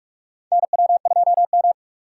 Morse code, heard as a keyed tone, about 47 wpm, 700 Hz. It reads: NW2M
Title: USA Call Signs - 4 Characters - 4x Character Spacing - 45wpm